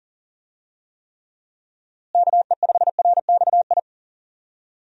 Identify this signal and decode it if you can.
Morse code, heard as a keyed tone, about 40 wpm, 700 Hz. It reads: KE5RXI
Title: USA Call Signs - 6 Characters - Repeated 3x - 40wpm